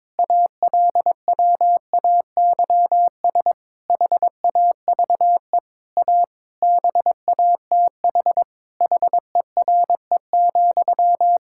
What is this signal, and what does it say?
Morse code, heard as a keyed tone, about 22 wpm, 700 Hz. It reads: ALWAYS HAVE A BATH HERE,
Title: Sentences from Top 1600 Words 22wpm